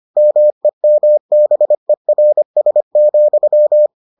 Morse code, heard as a keyed tone, about 25 wpm, 600 Hz. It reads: MEMBERS,